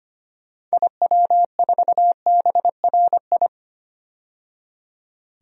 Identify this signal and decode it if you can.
Morse code, heard as a keyed tone, about 25 wpm, 690 Hz. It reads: IW4BRI
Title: DXCC-100 Call Signs - 6 Characters - 25wpm